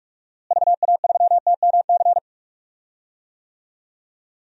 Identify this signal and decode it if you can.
Morse code, heard as a keyed tone, about 41 wpm, 700 Hz. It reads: VA3TMC
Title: DXCC-100 Call Signs - 6 Characters - Repeated 2x - 45wpm